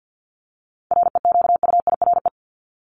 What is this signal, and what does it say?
Morse code, heard as a keyed tone, about 50 wpm, 700 Hz. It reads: REQUIRE
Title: Top 751-1000 Words 50wpm